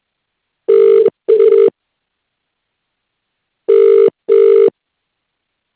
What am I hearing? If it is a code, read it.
Ringback tone (UK), 400 + 450 Hz.